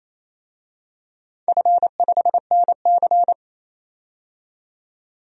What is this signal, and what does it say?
Morse code, heard as a keyed tone, about 27 wpm, 690 Hz. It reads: F5NC
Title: DXCC-100 Call Signs - 4 Characters - 28wpm